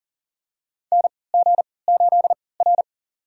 Morse code, heard as a keyed tone, about 40 wpm, spaced as for 23 wpm, 700 Hz. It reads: NG8R
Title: USA Call Signs - 4 Characters - 3x Character Spacing - 40wpm